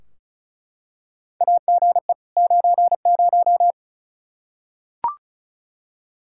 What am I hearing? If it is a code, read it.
Morse code, heard as a keyed tone, about 36 wpm, 700 Hz. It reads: AGE 90
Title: K1USN SST QSO Practice 35wpm